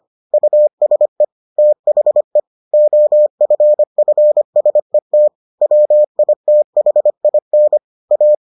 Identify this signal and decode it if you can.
Morse code, heard as a keyed tone, about 25 wpm, 600 Hz. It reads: USE THE OFFSET WITHIN A